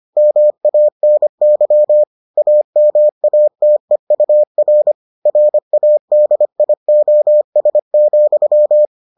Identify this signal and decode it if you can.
Morse code, heard as a keyed tone, about 25 wpm, 600 Hz. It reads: MANY AMATEUR RADIOS,